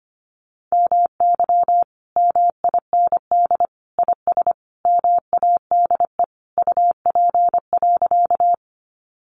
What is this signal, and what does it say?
Morse code, heard as a keyed tone, about 25 wpm, 700 Hz. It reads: MY MIND IS MADE UP.
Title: Sentences from Top 500 Words 25wpm